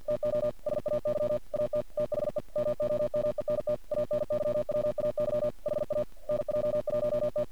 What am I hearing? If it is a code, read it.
Morse code, heard as a keyed tone, about 50 wpm, 600 Hz. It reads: TO SAY AT THE MOMENT ANYWAY HA NJ1T
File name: archived / CRSnet